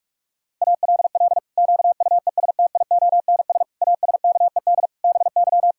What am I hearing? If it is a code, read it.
Morse code, heard as a keyed tone, about 46 wpm, 700 Hz. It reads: ALL QUESTIONS ASKED BY
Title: Pangram Challenge v2 - 45wpm